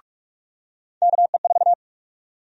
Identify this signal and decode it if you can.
Morse code, heard as a keyed tone, about 45 wpm, 700 Hz. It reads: KE4
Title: USA Call Sign Prefixes - 45wpm